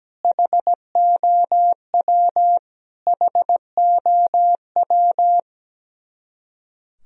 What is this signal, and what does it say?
Morse code, heard as a keyed tone, about 17 wpm, 700 Hz. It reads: HOW HOW
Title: Top 100 Words Repeated 17wpm